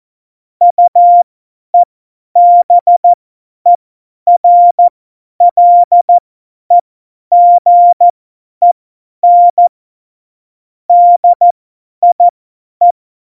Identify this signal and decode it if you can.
Morse code, heard as a keyed tone, about 14 wpm, spaced as for 10 wpm, 700 Hz. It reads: UEBERLEGEN DIE
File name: Morsekurs / Witze / Witz 11 August 2020, 50 F70